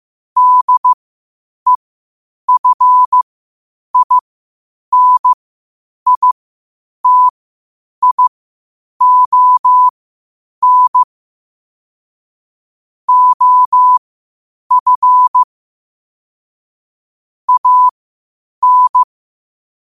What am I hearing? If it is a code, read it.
Morse code, heard as a keyed tone, about 15 wpm, spaced as for 8 wpm, 1000 Hz. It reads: DEFINITION OF AN
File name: qotdfiles / 8wpm